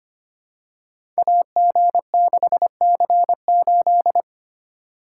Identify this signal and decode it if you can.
Morse code, heard as a keyed tone, about 25 wpm, 700 Hz. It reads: AG6C8
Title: Code Groups - 5 Letter:Numbers - 25wpm